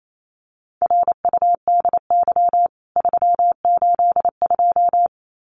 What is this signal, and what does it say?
Morse code, heard as a keyed tone, about 28 wpm, 700 Hz. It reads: RUDY 382
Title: CWT Exchange 28wpm